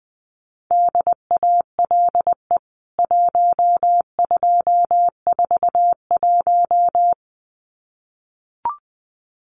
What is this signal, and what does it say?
Morse code, heard as a keyed tone, about 19 wpm, 700 Hz. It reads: DALE 1241
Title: CWT Exchange 20wpm